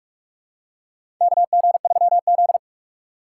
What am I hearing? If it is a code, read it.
Morse code, heard as a keyed tone, about 46 wpm, 700 Hz. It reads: KG3Z
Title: USA Call Signs - 4 Characters - 4x Character Spacing - 45wpm